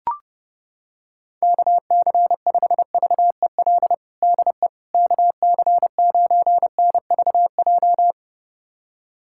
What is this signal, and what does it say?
Morse code, heard as a keyed tone, about 32 wpm, 700 Hz. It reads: KC5VEL DE KC9NVJ
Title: USA Callsign DE Callsign - 6 Characters - 30wpm